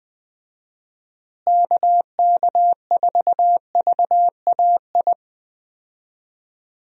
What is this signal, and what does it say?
Morse code, heard as a keyed tone, about 20 wpm, 700 Hz. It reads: KK4VAI
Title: USA Call Signs - 6 Characters - Repeated 2x - 20wpm